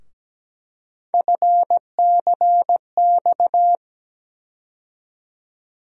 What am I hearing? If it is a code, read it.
Morse code, heard as a keyed tone, about 17 wpm, 700 Hz. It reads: FCX
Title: USA Call Sign Postfixes - 3 Letters 17wpm